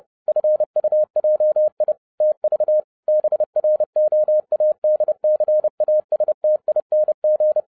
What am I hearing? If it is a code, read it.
Morse code, heard as a keyed tone, about 30 wpm, 610 Hz. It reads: FUJI TV BROADCASTING